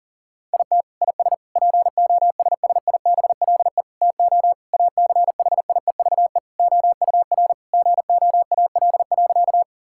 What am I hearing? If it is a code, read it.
Morse code, heard as a keyed tone, about 40 wpm, 700 Hz. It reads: IT IS POSSIBLE TO ACHIEVE OUR GOAL.
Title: Sentences from Top 1000 Words - Review Part 6 - 40wpm